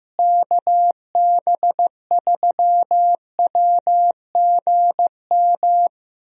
Morse code, heard as a keyed tone, about 16 wpm, 700 Hz. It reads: KB3WGM
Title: USA Call Signs - 6 Characters - Repeated 3x - 15wpm